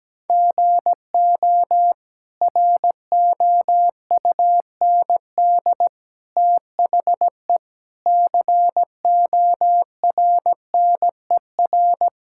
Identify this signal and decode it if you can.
Morse code, heard as a keyed tone, about 17 wpm, 700 Hz. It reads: GO ROUND THE CORNER
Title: Sentences from Top 1600 Words 17wpm